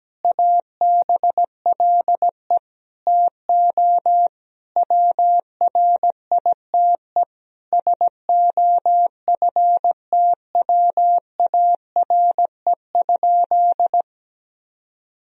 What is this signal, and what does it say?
Morse code, heard as a keyed tone, about 17 wpm, 700 Hz. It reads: ABLE TO WRITE SOFTWARE?
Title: Sentences from Top 1000 Words - Review Part 3 - 17wpm